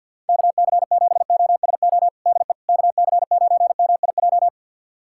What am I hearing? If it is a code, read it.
Morse code, heard as a keyed tone, about 52 wpm, 690 Hz. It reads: KC7OSO DE KC9MIJ
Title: USA Callsign DE Callsign - 6 Characters - 50wpm